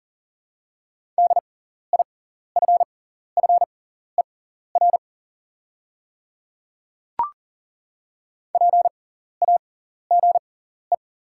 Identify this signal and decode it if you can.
Morse code, heard as a keyed tone, about 40 wpm, spaced as for 14 wpm, 690 Hz. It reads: DIFFER PAGE
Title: 6x Character Spacing - Top 300 Words - 40wpm